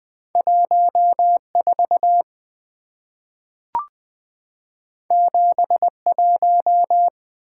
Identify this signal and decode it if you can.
Morse code, heard as a keyed tone, about 20 wpm, 690 Hz. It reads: 14 71
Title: Two Letter-Number Combinations - 20wpm